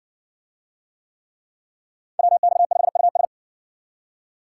Morse code, heard as a keyed tone, about 58 wpm, 700 Hz. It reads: WX5US